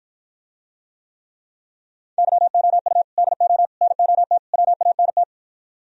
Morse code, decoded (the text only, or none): YOU DO NOT WANT